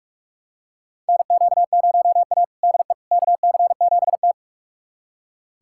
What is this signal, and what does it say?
Morse code, heard as a keyed tone, about 47 wpm, 700 Hz. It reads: NQ0A DE KC7T
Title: USA Callsign DE Callsign - 4 Characters - 45wpm